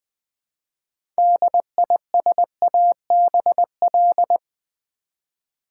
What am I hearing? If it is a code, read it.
Morse code, heard as a keyed tone, about 20 wpm, 700 Hz. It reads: DISABL
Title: Build A Word - 2 Letters At A Time - 8-Letter Words - 20wpm